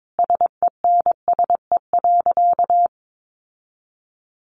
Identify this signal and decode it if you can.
Morse code, heard as a keyed tone, about 22 wpm, 700 Hz. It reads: SENSE.
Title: Sentences from Top 1300 Words 22wpm